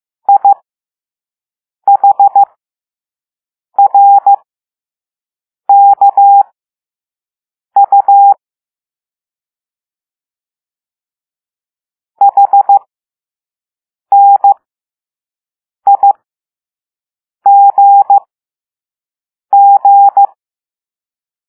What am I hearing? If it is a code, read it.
Morse code, heard as a keyed tone, about 15 wpm, spaced as for 5 wpm, 800 Hz. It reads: IHRKU HNIGG
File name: Morse / Level5-All-B